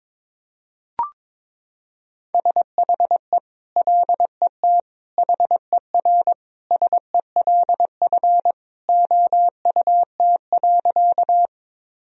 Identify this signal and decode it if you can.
Morse code, heard as a keyed tone, about 22 wpm, 700 Hz. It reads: SHE LET HERSELF OUT.
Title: Sentences from Top 1000 Words - Review Part 1 - 22wpm